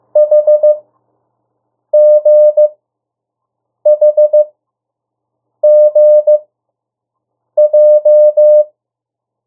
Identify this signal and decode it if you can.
Morse code, heard as a keyed tone, about 15 wpm, spaced as for 6 wpm, 600 Hz. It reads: HGHGJ
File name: urok 1